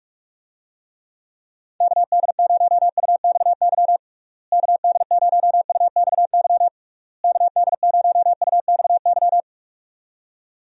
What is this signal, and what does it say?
Morse code, heard as a keyed tone, about 43 wpm, 700 Hz. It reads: KD0UXY KD0UXY KD0UXY
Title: USA Call Signs - 6 Characters - Repeated 3x - 45wpm